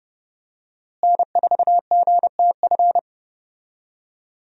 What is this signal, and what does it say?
Morse code, heard as a keyed tone, about 30 wpm, 700 Hz. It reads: N4GTF
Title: USA Call Signs - 5 Characters - Repeated 2x - 30wpm